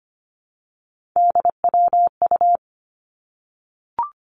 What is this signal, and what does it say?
Morse code, heard as a keyed tone, about 25 wpm, 700 Hz. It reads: DWU
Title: Three Challenging Letter-Number Combinations 25wpm